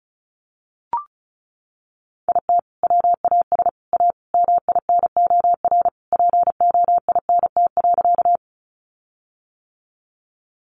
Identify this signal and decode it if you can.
Morse code, heard as a keyed tone, about 34 wpm, 700 Hz. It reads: IT WAS A MINOR POINT.
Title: Sentences from Top 1500 Words 35wpm